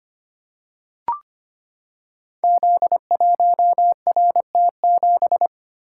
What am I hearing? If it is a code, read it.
Morse code, heard as a keyed tone, about 25 wpm, 700 Hz. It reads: Z1RT7
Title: Code Groups - 5 Letter:Numbers - 25wpm